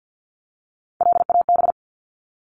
Morse code, heard as a keyed tone, about 50 wpm, 700 Hz. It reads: LAB
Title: Top 1751-2000 Words 50wpm